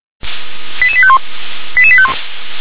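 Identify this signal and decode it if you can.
Five-tone selective call (ZVEI), heard as quick runs of five tones, tones of about 70 ms. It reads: 88851 88851
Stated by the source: ZVEI five-tone selective call received by SDR.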